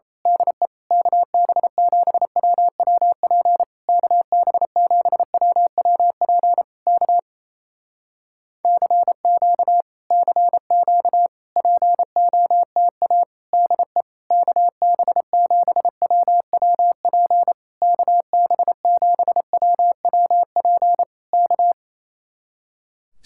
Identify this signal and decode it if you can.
Morse code, heard as a keyed tone, about 27 wpm, 700 Hz. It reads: DE KB7WWP KB7WWP K CQ CQ POTA DE KB7WWP KB7WWP K
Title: Speed-Racing Common QSO Phrases Encore - 28wpm